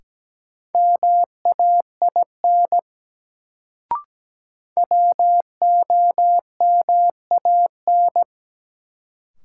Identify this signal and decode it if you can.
Morse code, heard as a keyed tone, about 17 wpm, 700 Hz. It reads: MAIN WOMAN
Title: Sentences from Top 1000 Words - Review Part 2 - New Word Drill - 17wpm